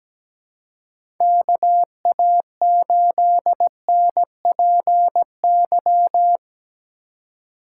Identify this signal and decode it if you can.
Morse code, heard as a keyed tone, about 17 wpm, 700 Hz. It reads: KA8NPY
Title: USA Call Signs - 4x Character Spacing - 17wpm